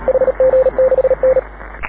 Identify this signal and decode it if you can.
Morse code, heard as a keyed tone, about 37 wpm, 520 Hz, with radio noise. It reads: HG6N
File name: AUDIO2019 / HG6N CW SSB MIX